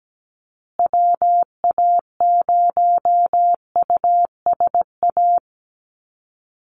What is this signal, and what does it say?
Morse code, heard as a keyed tone, about 17 wpm, 700 Hz. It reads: WA0USA
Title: Call Signs Repeated 3x - 17wpm